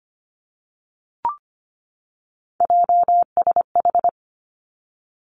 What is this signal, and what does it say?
Morse code, heard as a keyed tone, about 25 wpm, 700 Hz. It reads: JSH
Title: Three Challenging Letter-Number Combinations 25wpm